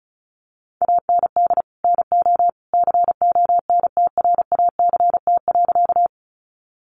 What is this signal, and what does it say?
Morse code, heard as a keyed tone, about 34 wpm, 700 Hz. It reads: AND NO CONTRACT.
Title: Sentences from Top 1400 Words 35wpm